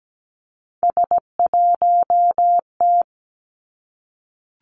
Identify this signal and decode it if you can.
Morse code, heard as a keyed tone, about 17 wpm, 700 Hz. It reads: S1T
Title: Three Challenging Letter-Number Combinations 17wpm